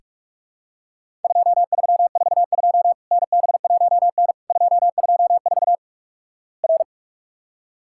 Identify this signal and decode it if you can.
Morse code, heard as a keyed tone, about 44 wpm, 700 Hz. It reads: 2342 NB1N 224 R
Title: Worked All Europe DX Contest - QTC Exchange - 45wpm